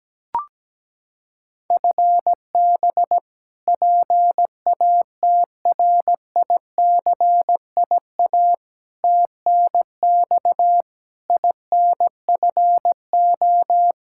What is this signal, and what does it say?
Morse code, heard as a keyed tone, about 17 wpm, 700 Hz. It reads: FB PATRICIA TNX INFO
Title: Practice QSO - Exchange 3 - More Info - 17wpm